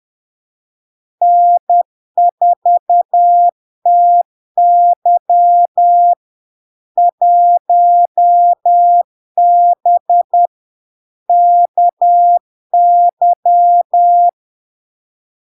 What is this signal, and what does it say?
Morse code, heard as a keyed tone, about 10 wpm, 700 Hz. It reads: N4TY 1B KY
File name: LARK-FD-2022-Log-in-CW-10wpm-700-Hz